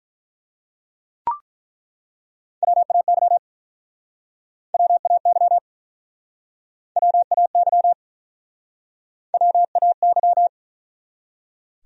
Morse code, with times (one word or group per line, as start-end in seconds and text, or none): 2.6-3.4 WAY
4.7-5.6 WAY
7.0-7.9 WAY
9.3-10.5 WAY